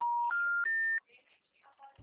Special information tone.
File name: out- -2013- - - .60019